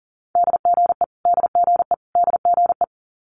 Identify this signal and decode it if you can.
Morse code, heard as a keyed tone, about 38 wpm, 700 Hz. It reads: DGE DGE DGE
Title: Multi-letter Phonemes Repeated 3x 40wpm